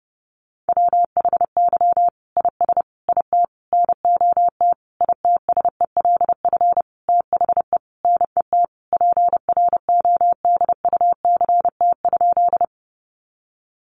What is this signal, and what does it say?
Morse code, heard as a keyed tone, about 30 wpm, 700 Hz. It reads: WHY IS IT NOT ITSELF THE NET PRODUCT?
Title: Sentences from Top 1400 Words 30wpm